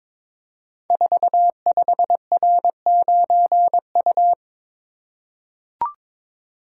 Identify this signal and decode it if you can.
Morse code, heard as a keyed tone, about 22 wpm, 700 Hz. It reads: 45R9U
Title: Code Groups - 5 Letter:Numbers - 22wpm